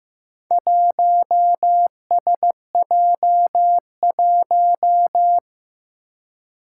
Morse code, heard as a keyed tone, about 15 wpm, 700 Hz. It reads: 1SJ1
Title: Four Challenging Letter-Number Combinations 15wpm